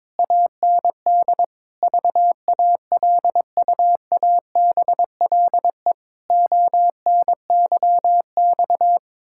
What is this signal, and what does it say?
Morse code, heard as a keyed tone, about 22 wpm, 700 Hz. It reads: AND VALUABLE ONYX
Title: Pangram Challenge v2 - 22wpm